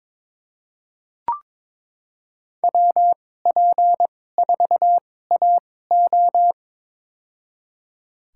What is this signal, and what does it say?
Morse code, heard as a keyed tone, about 22 wpm, spaced as for 16 wpm, 700 Hz. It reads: WP4AO
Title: USA Call Signs - 5 Characters - 2x Character Spacing - 22wpm